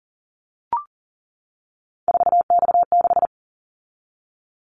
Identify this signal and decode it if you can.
Morse code, heard as a keyed tone, about 42 wpm, 700 Hz. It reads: 4X6
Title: Callsign Prefixes for Top 50 DXCC - 40wpm